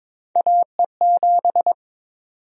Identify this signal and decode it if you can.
Morse code, heard as a keyed tone, about 22 wpm, 700 Hz. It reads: AE7
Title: USA Call Sign Prefixes - 22wpm